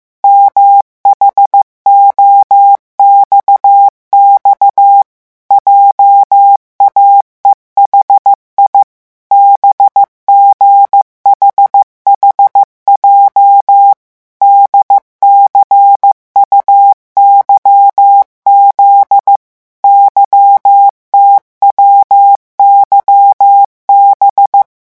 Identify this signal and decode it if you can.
Morse code, heard as a keyed tone, about 15 wpm, 800 Hz. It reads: MHOXX JAEHI BGHHJ DCUYZ YTWYB
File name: morse mp3 / revision11 lettres vitesse 14